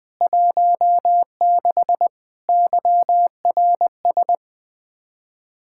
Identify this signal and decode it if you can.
Morse code, heard as a keyed tone, about 21 wpm, 700 Hz. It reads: 16 YRS
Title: Practice QSO - Exchange 3 - More Info - 20wpm